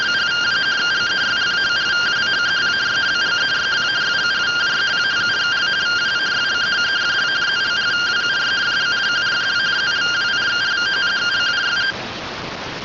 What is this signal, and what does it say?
Two-tone teleprinter signal, about 170 Hz shift.